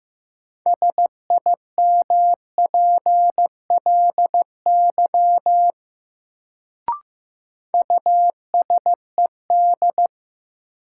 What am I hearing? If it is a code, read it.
Morse code, heard as a keyed tone, about 15 wpm, 700 Hz. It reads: SIMPLY USED
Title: Sentences from Top 600 Words - New Word Drill - 15wpm